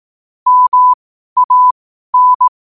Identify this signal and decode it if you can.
Morse code, heard as a keyed tone, about 18 wpm, spaced as for 12 wpm, 1000 Hz. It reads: MAN
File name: qotdfiles / 12wpm